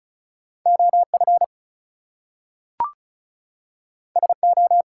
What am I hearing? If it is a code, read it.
Morse code, heard as a keyed tone, about 36 wpm, 700 Hz. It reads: OF SO
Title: Two Letter Words - 35wpm